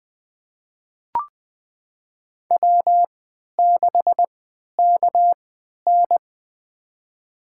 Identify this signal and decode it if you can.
Morse code, heard as a keyed tone, about 20 wpm, spaced as for 11 wpm, 700 Hz. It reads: W6KN
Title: USA Call Signs - 4 Characters - 3x Character Spacing - 20wpm